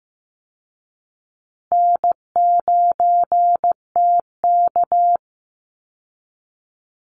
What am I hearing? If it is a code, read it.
Morse code, heard as a keyed tone, about 15 wpm, 700 Hz. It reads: N9TK
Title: Easy Call Signs Repeated 2x - 15wpm